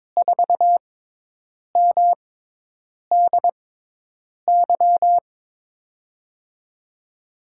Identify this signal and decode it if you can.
Morse code, heard as a keyed tone, about 22 wpm, spaced as for 8 wpm, 700 Hz. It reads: 4MDY
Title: USA Call Signs - 6 Characters - 6x Character Spacing - 22wpm